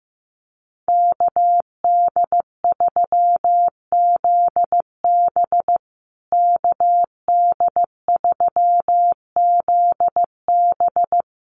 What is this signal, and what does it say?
Morse code, heard as a keyed tone, about 15 wpm, 700 Hz. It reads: KD3ZB KD3ZB
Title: USA Call Signs Repeated 2x 15wpm